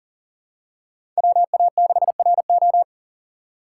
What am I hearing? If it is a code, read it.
Morse code, heard as a keyed tone, about 40 wpm, 700 Hz. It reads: WA6RO
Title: USA Call Signs - 5 Characters - 6x Character Spacing - 40wpm